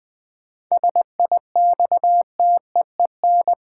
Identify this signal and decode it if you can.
Morse code, heard as a keyed tone, about 20 wpm, 700 Hz. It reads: SIXTEEN
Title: 7 Letter Words - First 3 Letters Given - 20wpm